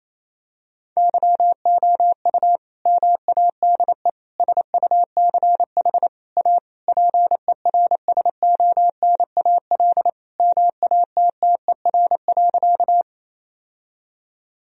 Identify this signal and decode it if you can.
Morse code, heard as a keyed tone, about 28 wpm, 700 Hz. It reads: YOU MADE SUCH A PERSONAL MATTER.
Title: Sentences from Top 1000 Words - Review Part 4 - 28wpm